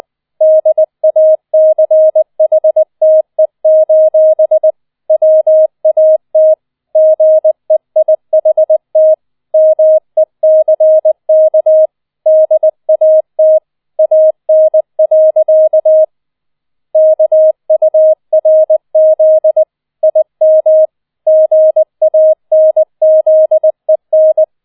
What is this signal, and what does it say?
Morse code, heard as a keyed tone, about 20 wpm, 630 Hz. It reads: DACHTE: WAT GEIHT MECK DAT AN. KURZ IM GANZEN